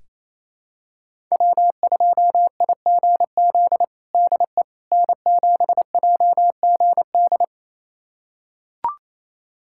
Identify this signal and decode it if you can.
Morse code, heard as a keyed tone, about 28 wpm, 700 Hz. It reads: W2IGZ DE N7JGD
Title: USA Callsign DE Callsign - 5 Characters - 28wpm